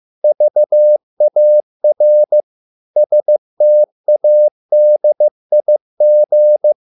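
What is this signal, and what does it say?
Morse code, heard as a keyed tone, about 15 wpm, 600 Hz. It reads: VAR STADIG
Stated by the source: Morse code at a slow 15 wpm from CW med Gnister.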